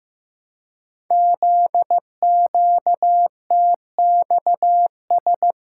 Morse code, heard as a keyed tone, about 15 wpm, 700 Hz. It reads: ZQTXS
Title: Code Groups - 5 Letters - 15wpm